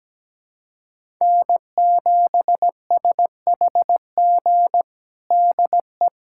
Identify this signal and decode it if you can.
Morse code, heard as a keyed tone, about 17 wpm, 700 Hz. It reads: N7SHG DE
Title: USA Callsign DE Callsign - 5 Characters - 17wpm